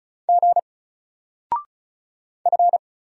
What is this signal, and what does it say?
Morse code, heard as a keyed tone, about 33 wpm, 700 Hz. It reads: G F
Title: Single Letter-Number 35wpm